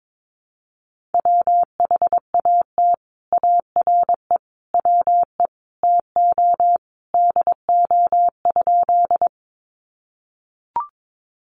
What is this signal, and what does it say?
Morse code, heard as a keyed tone, about 22 wpm, 700 Hz. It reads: WHAT ARE WE TO DO?
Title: Sentences from Top 100 Words 22wpm